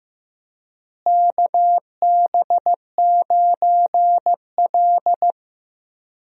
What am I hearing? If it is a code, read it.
Morse code, heard as a keyed tone, about 16 wpm, 700 Hz. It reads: KB9L
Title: USA Call Signs - 4 Characters - Repeated 2x - 15wpm